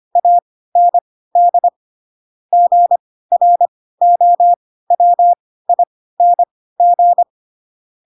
Morse code, heard as a keyed tone, about 25 wpm, spaced as for 16 wpm, 700 Hz. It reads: AND GROWING
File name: Pinocchio / Pinocchio ch2 10 25 ik1hge